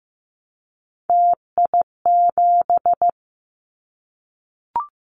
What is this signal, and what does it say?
Morse code, heard as a keyed tone, about 15 wpm, 700 Hz. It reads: TI7
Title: Callsign Prefixes for Top 100 DXCC - 15wpm